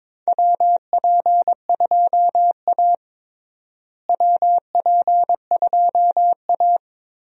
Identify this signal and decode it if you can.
Morse code, heard as a keyed tone, about 22 wpm, 700 Hz. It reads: WP2A WP2A
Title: USA Call Signs - 4 Characters - Repeated 3x - 22wpm